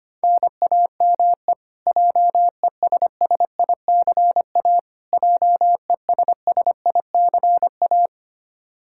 Morse code, heard as a keyed tone, about 25 wpm, 700 Hz. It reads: NAME JESSICA JESSICA
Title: Practice QSO - Exchange 1 - Beginning - 25wpm